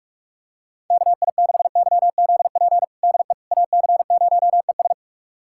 Morse code, heard as a keyed tone, about 46 wpm, 700 Hz. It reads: KI6YZP DE AC0ES
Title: USA Callsign DE Callsign 45wpm